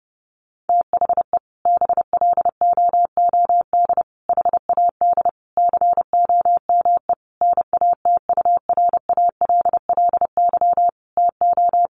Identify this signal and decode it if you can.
Morse code, heard as a keyed tone, about 30 wpm, 700 Hz. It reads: THE BLOOD HAD COME NATURALLY TO
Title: Sentences from Top 1300 Words 30wpm